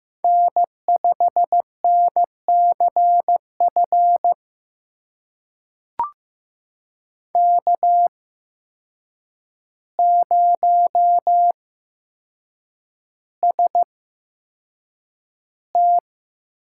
Morse code, heard as a keyed tone, about 15 wpm, spaced as for 7 wpm, 700 Hz. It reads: N5NCF K0ST
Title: USA Call Signs - 8x Character Spacing - 15wpm